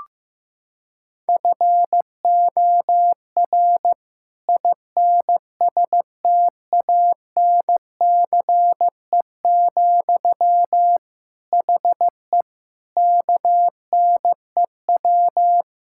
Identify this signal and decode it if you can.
Morse code, heard as a keyed tone, about 15 wpm, 690 Hz. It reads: FOR INSTANCE, HE KNEW
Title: Sentences from Top 1000 Words - Review Part 5 - 15wpm